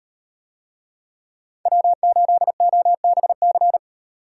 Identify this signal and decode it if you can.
Morse code, heard as a keyed tone, about 35 wpm, 700 Hz. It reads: W8OBC